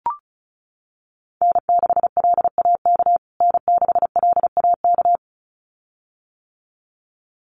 Morse code, heard as a keyed tone, about 34 wpm, 700 Hz. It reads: N6LAK N6LAK
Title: USA Call Signs Repeated 2x 35wpm